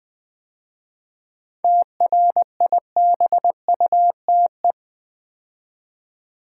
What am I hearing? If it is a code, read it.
Morse code, heard as a keyed tone, about 20 wpm, 700 Hz. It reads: TRIBUTE
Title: 7 Letter Words - First 3 Letters Given - 20wpm